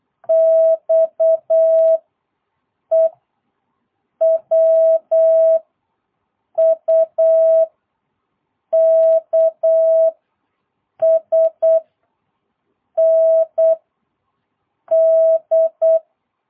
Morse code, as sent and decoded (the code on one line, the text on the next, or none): -..- / . / .-- / ..- / -.- / ... / -. / -..
X E W U K S N D